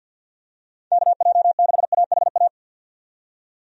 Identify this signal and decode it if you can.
Morse code, heard as a keyed tone, about 51 wpm, 700 Hz. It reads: KJ6AHA
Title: USA Call Signs - 6 Characters - 50wpm